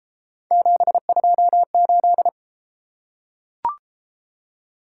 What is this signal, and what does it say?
Morse code, heard as a keyed tone, about 31 wpm, 700 Hz. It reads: 728